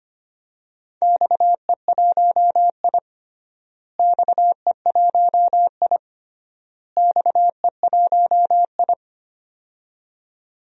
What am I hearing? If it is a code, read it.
Morse code, heard as a keyed tone, about 25 wpm, 700 Hz. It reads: XE1I XE1I XE1I
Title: DXCC-100 Call Signs - 4 Characters - Repeated 3x - 25wpm